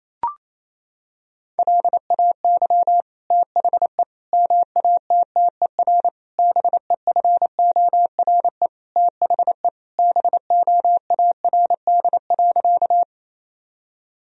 Morse code, heard as a keyed tone, about 28 wpm, 700 Hz. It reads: LAY THE MATTER BEFORE THE BOARD.
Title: Sentences from Top 1000 Words - Review Part 4 - 28wpm